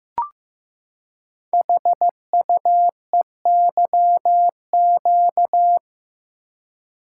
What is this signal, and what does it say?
Morse code, heard as a keyed tone, about 15 wpm, 700 Hz. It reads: HUEYQ
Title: Code Groups - 5 Letters - 15wpm